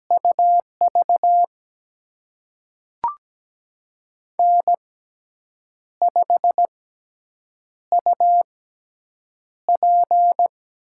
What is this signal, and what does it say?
Morse code, heard as a keyed tone, about 17 wpm, spaced as for 7 wpm, 700 Hz. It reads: UV N5UP
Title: USA Call Signs - 6x Character Spacing - 17wpm